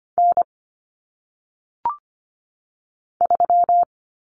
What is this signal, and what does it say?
Morse code, heard as a keyed tone, about 25 wpm, 700 Hz. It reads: N 3
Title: Common Cut Numbers - 1-Digit 25wpm